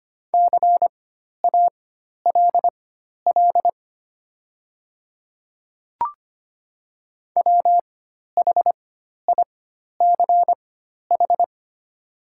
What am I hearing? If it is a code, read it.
Morse code, heard as a keyed tone, about 25 wpm, spaced as for 12 wpm, 700 Hz. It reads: CALL WHICH
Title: 4x Character Spacing - Top 100 Words - 25wpm